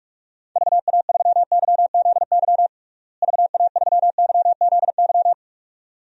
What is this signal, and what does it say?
Morse code, heard as a keyed tone, about 46 wpm, 700 Hz. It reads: VA3YZY VA3YZY
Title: DXCC-100 Call Signs - 6 Characters - Repeated 2x - 45wpm